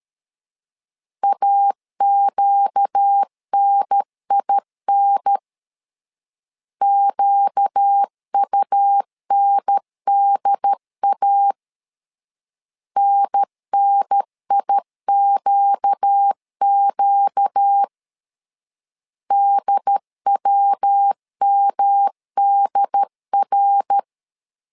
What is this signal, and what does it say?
Morse code, heard as a keyed tone, about 13 wpm, 800 Hz. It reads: AQNIN QUNDA NNIQQ DWMDR
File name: LEZIONI / 4 Q 12wpm 1833